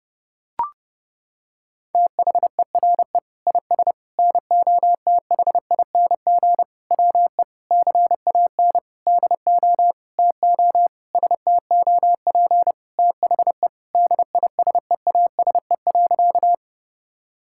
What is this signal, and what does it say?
Morse code, heard as a keyed tone, about 28 wpm, 700 Hz. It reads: THERE IS NOTHING WE CAN DO TO STOP THE DISEASE.
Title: Sentences from Top 1000 Words - Review Part 6 - 30wpm